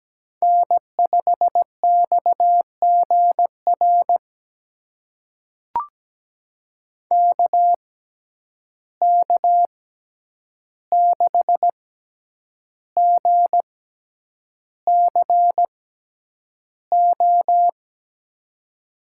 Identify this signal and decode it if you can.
Morse code, heard as a keyed tone, about 17 wpm, spaced as for 8 wpm, 700 Hz. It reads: N5XGR KK6GCO
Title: USA Call Signs - 6x Character Spacing - 17wpm